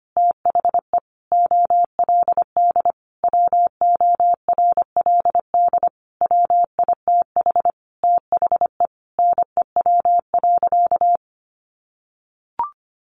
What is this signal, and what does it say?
Morse code, heard as a keyed tone, about 25 wpm, 700 Hz. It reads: THE OLD WORLD WITH THE NEW.
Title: Sentences from Top 1300 Words 25wpm